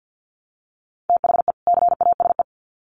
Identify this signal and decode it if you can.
Morse code, heard as a keyed tone, about 50 wpm, 700 Hz. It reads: THE CASE
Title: Sets of 2 Words 50wpm